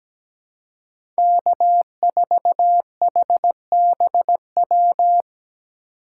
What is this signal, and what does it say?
Morse code, heard as a keyed tone, about 17 wpm, 700 Hz. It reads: K4HBW
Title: USA Call Signs - 5 Characters - 3x Character Spacing - 17wpm